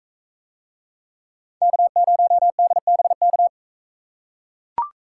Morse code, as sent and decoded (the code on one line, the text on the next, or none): -.- ----- -.. -... -.-
K0DBK